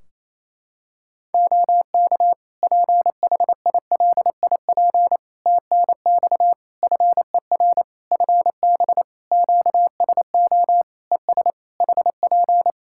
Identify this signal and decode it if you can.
Morse code, heard as a keyed tone, about 28 wpm, 700 Hz. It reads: OK PHILIP TNX FER FB QSO ES HP
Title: K1USN SST QSO Practice 28wpm